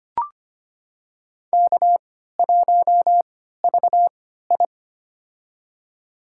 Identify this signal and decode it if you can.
Morse code, heard as a keyed tone, about 25 wpm, spaced as for 14 wpm, 700 Hz. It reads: K1VI
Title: USA Call Signs - 4 Characters - 3x Character Spacing - 25wpm